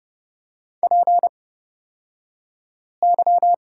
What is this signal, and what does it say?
Morse code, heard as a keyed tone, about 30 wpm, spaced as for 5 wpm, 700 Hz. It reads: PY
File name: Lesson 37d - Letter x - Two Characters 30wpm